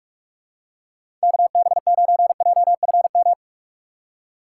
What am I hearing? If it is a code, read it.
Morse code, heard as a keyed tone, about 46 wpm, 700 Hz. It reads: KB9JFM